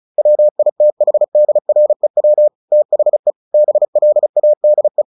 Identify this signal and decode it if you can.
Morse code, heard as a keyed tone, about 34 wpm, 600 Hz. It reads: WITHDREW THE BLADE
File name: the old man and the sea 2 35wpm